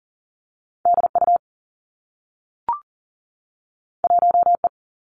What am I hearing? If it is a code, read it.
Morse code, heard as a keyed tone, about 40 wpm, 700 Hz. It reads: D U 1 E
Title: Two Challenging Letter-Number Combinations 40wpm